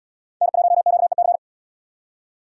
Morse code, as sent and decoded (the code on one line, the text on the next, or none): -. .---- --.- . -.-.
N1QEC